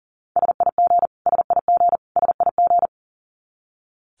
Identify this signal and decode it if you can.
Morse code, heard as a keyed tone, about 40 wpm, 700 Hz. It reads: SIG SIG SIG
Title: Abbreviations Repeated 3x - 40wpm